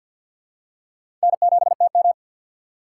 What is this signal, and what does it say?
Morse code, heard as a keyed tone, about 50 wpm, 700 Hz. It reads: N7TM
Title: USA Call Signs - 4 Characters - 2x Character Spacing - 50wpm